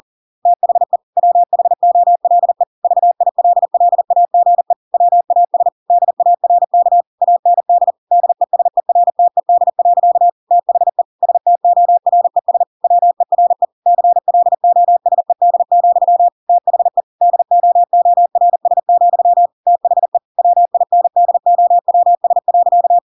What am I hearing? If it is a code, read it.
Morse code, heard as a keyed tone, about 40 wpm, 700 Hz. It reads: THE WHOLE VILLAGE WAS DARK AND DESERTED. THE STORES WERE CLOSED, THE DOORS, THE WINDOWS.